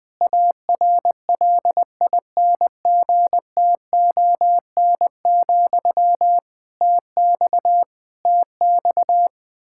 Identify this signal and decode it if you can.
Morse code, heard as a keyed tone, about 20 wpm, 700 Hz. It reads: ARLINGTON, TX TX
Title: Practice QSO - Exchange 1 - Beginning - 20wpm